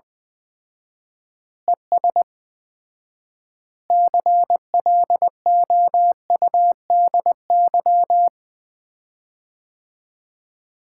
Morse code, heard as a keyed tone, about 20 wpm, 700 Hz. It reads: ES CLOUDY
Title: Common QSO Phrases - 4x Word-Spacing - 20wpm